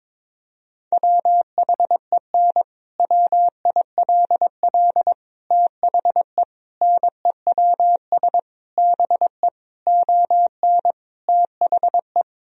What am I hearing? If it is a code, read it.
Morse code, heard as a keyed tone, about 22 wpm, 700 Hz. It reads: WHEN WILL THE NEWS BE ON THE
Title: Sentences from Top 1000 Words - Review Part 3 - 22wpm